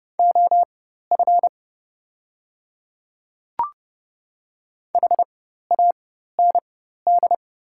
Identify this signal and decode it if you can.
Morse code, heard as a keyed tone, about 29 wpm, spaced as for 14 wpm, 700 Hz. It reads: OF HAND
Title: 4x Character Spacing - Top 500 Words - 30wpm